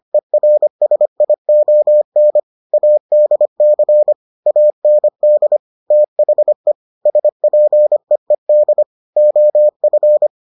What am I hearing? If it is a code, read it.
Morse code, heard as a keyed tone, about 25 wpm, 600 Hz. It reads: ERSION ADC AND THE SPEED OF